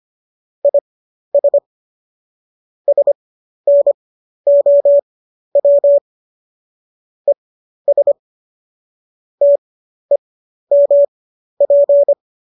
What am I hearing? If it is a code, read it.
Morse code, heard as a keyed tone, about 25 wpm, spaced as for 12 wpm, 570 Hz. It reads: IS SNOW ES TEMP